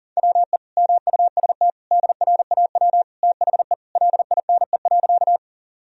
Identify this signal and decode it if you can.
Morse code, heard as a keyed tone, about 40 wpm, 700 Hz. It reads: WE MUST DRAW THE LINE.
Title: Sentences from Top 1000 Words - Review Part 1 - 40wpm